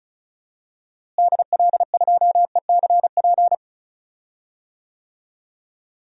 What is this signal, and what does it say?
Morse code, heard as a keyed tone, about 36 wpm, 700 Hz. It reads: DL2ECP
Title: DXCC-100 Call Signs - 6 Characters - Repeated 3x - 35wpm